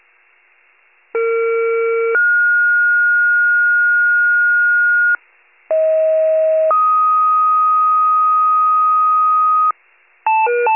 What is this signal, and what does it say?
Two-tone page, heard as a one-second tone, then a three-second tone, about 460 then 1430 Hz, about 640 then 1250 Hz, with noise. It is heard from 1.1 to 9.7 s.
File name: pager / 38 FIRE / wcalert